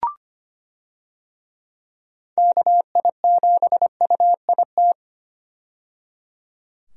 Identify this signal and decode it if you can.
Morse code, heard as a keyed tone, about 25 wpm, 700 Hz. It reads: KI7UIT
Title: USA Call Signs - 6 Characters - 25wpm